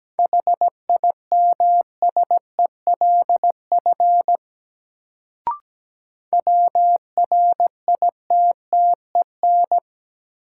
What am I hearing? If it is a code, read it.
Morse code, heard as a keyed tone, about 17 wpm, 700 Hz. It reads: HIMSELF WRITTEN
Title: Sentences from Top 1500 Words - New Word Drill - 17wpm